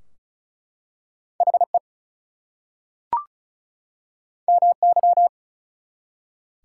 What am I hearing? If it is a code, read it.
Morse code, heard as a keyed tone, about 33 wpm, 700 Hz. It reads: HE MY
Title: Two Letter Words - 35wpm